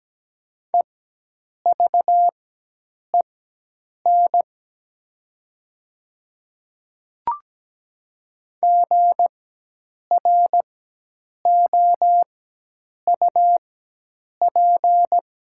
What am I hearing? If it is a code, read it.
Morse code, heard as a keyed tone, about 17 wpm, spaced as for 8 wpm, 700 Hz. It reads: EVEN GROUP
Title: 4x Character Spacing - Top 500 Words - 17wpm